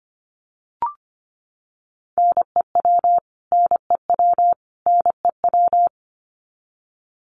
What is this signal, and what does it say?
Morse code, heard as a keyed tone, about 25 wpm, 700 Hz. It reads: NEW NEW NEW
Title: Top 200 Words Repeated 25wpm